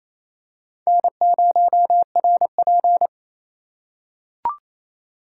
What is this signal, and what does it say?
Morse code, heard as a keyed tone, about 28 wpm, 690 Hz. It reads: N0RP
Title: USA Call Signs - 2x Character Spacing - 28wpm